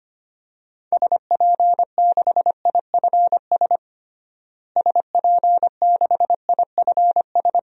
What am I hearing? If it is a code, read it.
Morse code, heard as a keyed tone, about 25 wpm, 700 Hz. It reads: SP6IFS SP6IFS
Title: DXCC-100 Call Signs - 6 Characters - Repeated 2x - 25wpm